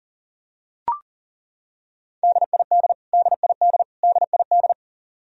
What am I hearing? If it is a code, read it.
Morse code, heard as a keyed tone, about 40 wpm, 700 Hz. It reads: DID DID DID
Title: Top 100 Words Repeated 40wpm